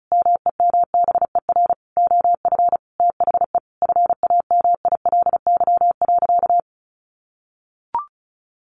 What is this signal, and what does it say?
Morse code, heard as a keyed tone, about 34 wpm, 700 Hz. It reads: MEMBER OF THE FAMILY.
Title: Sentences from Top 1100 Words 35wpm